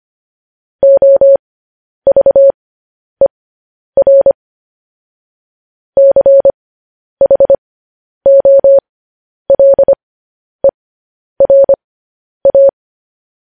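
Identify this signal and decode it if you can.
Morse code, heard as a keyed tone, about 25 wpm, spaced as for 10 wpm, 550 Hz. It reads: OVER CHOLERA
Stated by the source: Morse code news headlines practice, at 10 WPM.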